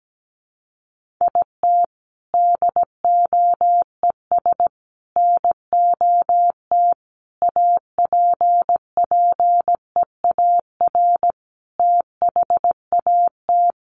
Morse code, heard as a keyed tone, about 17 wpm, 700 Hz. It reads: IT DOES NOT APPEAR THAT
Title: Sentences from Top 1400 Words 17wpm